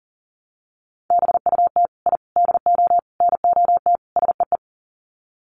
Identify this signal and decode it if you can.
Morse code, heard as a keyed tone, about 40 wpm, 700 Hz. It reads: BUT I DO NOT SEE
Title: Sets of 5 Words Encore 40wpm